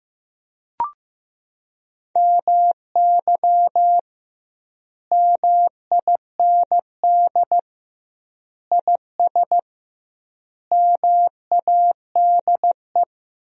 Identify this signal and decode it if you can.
Morse code, heard as a keyed tone, about 15 wpm, 700 Hz. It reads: MY MIND IS MADE
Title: 2x Word-Spacing - Sentences from Top 500 Words - 15wpm